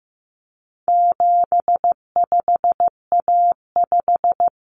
Morse code, heard as a keyed tone, about 15 wpm, 700 Hz. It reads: 75A5
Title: Common Cut Numbers - 4-Digit 15wpm